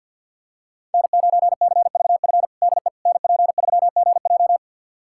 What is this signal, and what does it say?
Morse code, heard as a keyed tone, about 52 wpm, 700 Hz. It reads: N9CVF DE NP3GJ
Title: USA Callsign DE Callsign - 5 Characters - 50wpm